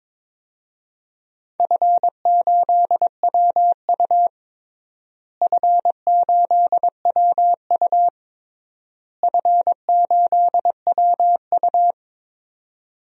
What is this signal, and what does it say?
Morse code, heard as a keyed tone, about 22 wpm, 700 Hz. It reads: F8WU F8WU F8WU
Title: DXCC-100 Call Signs - 4 Characters - Repeated 3x - 22wpm